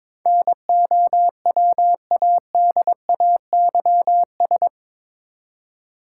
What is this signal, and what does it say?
Morse code, heard as a keyed tone, about 22 wpm, 700 Hz. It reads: NOWADAYS
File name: practice / 22wpm / Eight Letter Words - 22wpm